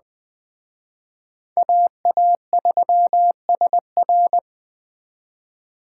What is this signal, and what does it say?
Morse code, heard as a keyed tone, about 21 wpm, 700 Hz. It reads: AA3SR